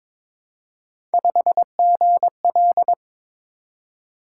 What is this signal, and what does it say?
Morse code, heard as a keyed tone, about 22 wpm, 700 Hz. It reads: 5GL
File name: practice / 22wpm / Three Letter-Number Combinations - 22wpm